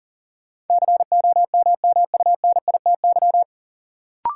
Morse code, heard as a keyed tone, about 40 wpm, 700 Hz. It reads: COMMUNITY
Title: Sentences from Top 1200 Words - New Word Drill Alt - 40wpm